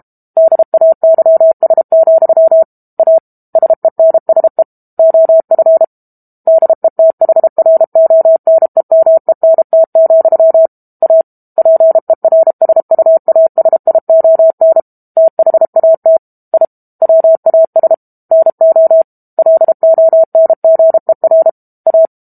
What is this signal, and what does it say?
Morse code, heard as a keyed tone, about 32 wpm, 650 Hz. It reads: DAYS, A SENSE OF DETHRONEMENT, A PERSUASION THAT I WAS NO LONGER A
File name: Book Two / 06-Chapter 6 - 32 WPM